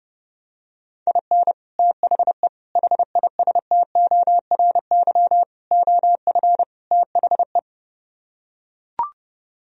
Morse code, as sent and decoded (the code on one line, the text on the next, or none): .. -. / - .... / .... .. ... - --- .-. -.-- / --- ..-. / - .... .
IN THE HISTORY OF THE